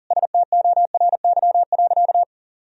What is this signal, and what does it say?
Morse code, heard as a keyed tone, about 40 wpm, 700 Hz. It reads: STORY.
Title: Sentences from Top 1000 Words - Review Part 1 - 40wpm